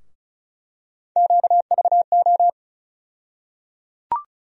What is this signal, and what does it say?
Morse code, heard as a keyed tone, about 36 wpm, 700 Hz. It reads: QVO
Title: USA Call Sign Postfixes - 3 Letters 35wpm